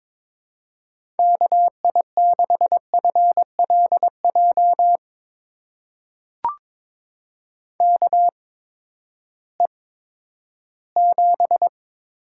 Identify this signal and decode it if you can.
Morse code, heard as a keyed tone, about 22 wpm, spaced as for 13 wpm, 700 Hz. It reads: KI6FLJ KE7
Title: USA Call Signs - 6 Characters - 8x Character Spacing - 22wpm